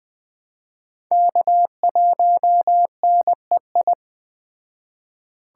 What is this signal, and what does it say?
Morse code, heard as a keyed tone, about 20 wpm, 700 Hz. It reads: K1NEI
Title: Speed Racing - USA Call Signs 17wpm